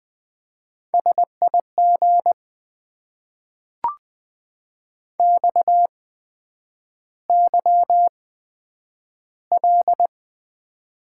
Morse code, heard as a keyed tone, about 20 wpm, spaced as for 9 wpm, 700 Hz. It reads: SIG XYL
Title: Top 90 QSO Elements - 8x Character Spacing - 20wpm